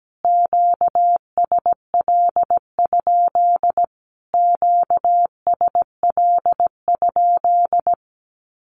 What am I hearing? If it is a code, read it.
Morse code, heard as a keyed tone, about 17 wpm, 700 Hz. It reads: QSL? QSL?
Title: Q-Codes Repeated 3x 17wpm